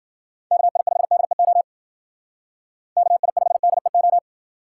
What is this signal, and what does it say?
Morse code, heard as a keyed tone, about 52 wpm, 700 Hz. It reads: KI5DEO KI5DEO